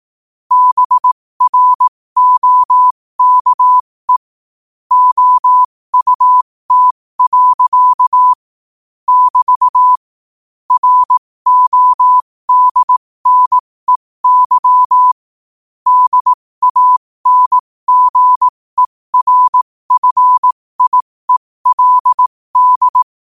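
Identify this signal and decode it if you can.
Morse code, heard as a keyed tone, about 18 wpm, spaced as for 15 wpm, 1000 Hz. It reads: BROKE OUT. <BT> RODNEY DANGERFIELD
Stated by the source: Morse practice sent at 15 wpm.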